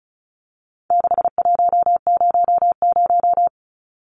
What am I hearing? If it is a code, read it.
Morse code, heard as a keyed tone, about 34 wpm, 700 Hz. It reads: 6100
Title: Numbers - 4-Digit 35wpm